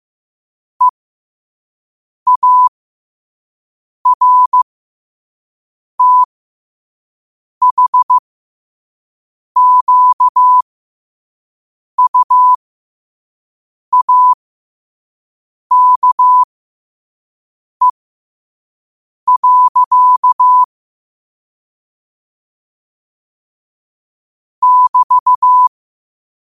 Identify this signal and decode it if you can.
Morse code, heard as a keyed tone, about 15 wpm, spaced as for 5 wpm, 1000 Hz. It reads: EARTHQUAKE. <BT>
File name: 5wpm